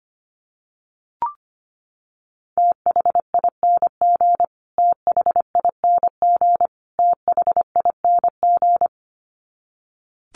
Morse code, heard as a keyed tone, about 25 wpm, 700 Hz. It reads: THING THING THING
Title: Top 500 Words Repeated 25wpm